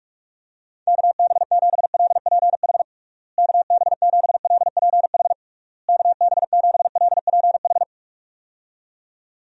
Morse code, heard as a keyed tone, about 46 wpm, 700 Hz. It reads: KB7LPH KB7LPH KB7LPH
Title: USA Call Signs - 6 Characters - Repeated 3x - 45wpm